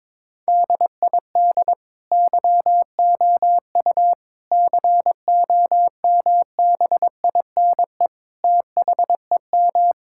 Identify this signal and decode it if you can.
Morse code, heard as a keyed tone, about 22 wpm, 700 Hz. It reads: DID YOU COMBINE THEM
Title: Sentences from Top 1600 Words 22wpm